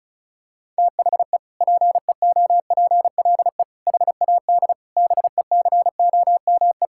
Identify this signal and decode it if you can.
Morse code, heard as a keyed tone, about 34 wpm, 700 Hz. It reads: THE PEOPLE HAD BECOME
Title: Sentences from Top 1000 Words - Review Part 6 - 35wpm